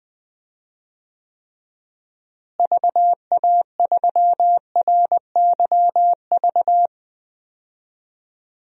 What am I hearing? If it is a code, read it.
Morse code, heard as a keyed tone, about 20 wpm, 700 Hz. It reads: VA3RYV
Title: DXCC-100 Call Signs - 6 Characters - Repeated 2x - 20wpm